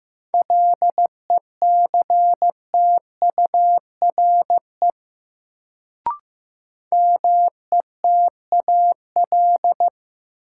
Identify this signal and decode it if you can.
Morse code, heard as a keyed tone, about 15 wpm, 700 Hz. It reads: LECTURE METAL
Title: Sentences from Top 1600 Words - New Word Drill Alt - 15wpm